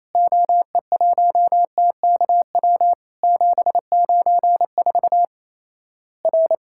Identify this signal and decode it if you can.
Morse code, heard as a keyed tone, about 27 wpm, 700 Hz. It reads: OE1TKW 794 R
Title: Worked All Europe DX Contest - QTC Exchange - 28wpm